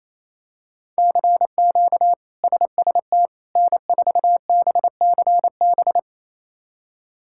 Morse code, heard as a keyed tone, about 28 wpm, 700 Hz. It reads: CQ SST N4BCB
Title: K1USN SST QSO Practice - 28wpm